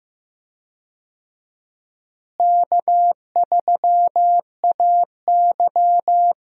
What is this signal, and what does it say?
Morse code, heard as a keyed tone, about 15 wpm, 700 Hz. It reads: K3AY
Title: USA Call Signs - 4 Characters - Repeated 2x - 15wpm